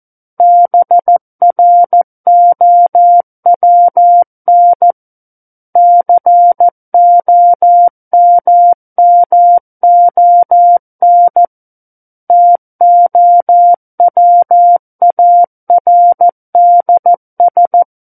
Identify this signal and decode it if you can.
Morse code, heard as a keyed tone, about 14 wpm, 700 Hz. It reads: BROWN COMMON TOWARDS
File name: Book One / 05-Chapter 5 - 14 WPM